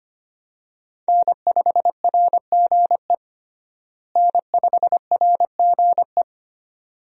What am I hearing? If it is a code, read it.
Morse code, heard as a keyed tone, about 25 wpm, 700 Hz. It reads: N5RGE N5RGE
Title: USA Call Signs - 5 Characters - Repeated 2x - 25wpm